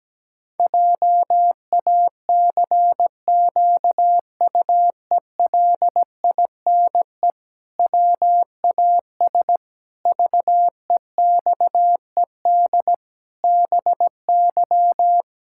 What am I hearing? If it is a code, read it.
Morse code, heard as a keyed tone, about 17 wpm, 700 Hz. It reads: JACQUELINE WAS VEXED BY
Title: Pangram Challenge v2 - 17wpm